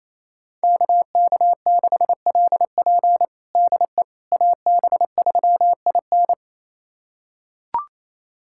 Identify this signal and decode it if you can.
Morse code, heard as a keyed tone, about 28 wpm, 700 Hz. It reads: KK6LP DE AB3IN
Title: USA Callsign DE Callsign - 5 Characters - 28wpm